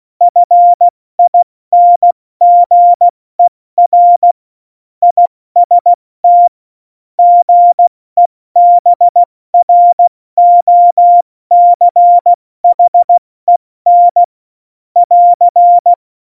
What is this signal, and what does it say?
Morse code, heard as a keyed tone, about 16 wpm, 700 Hz. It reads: FINGER IST GEBROCHEN <AR>
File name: Morsekurs / Witze / Witz 09. Januar 2018, 70 F80